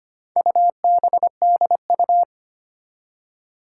Morse code, heard as a keyed tone, about 25 wpm, 700 Hz. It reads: UBDU
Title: Four Challenging Letter-Number Combinations 25wpm